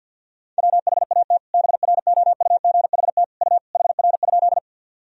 Morse code, heard as a keyed tone, about 48 wpm, 700 Hz. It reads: WHAT BROUGHT U HR?
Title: Sentences from Top 500 Words and Abbreviations 50wpm